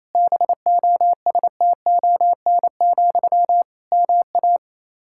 Morse code, heard as a keyed tone, about 28 wpm, 700 Hz. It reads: BOSTON, MA
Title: Practice QSO - Exchange 1 - Beginning - 28wpm